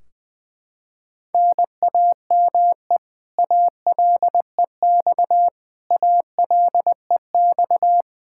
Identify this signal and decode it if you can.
Morse code, heard as a keyed tone, about 21 wpm, 700 Hz. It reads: NAME ALEX ALEX
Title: Practice QSO - Exchange 1 - Beginning - 20wpm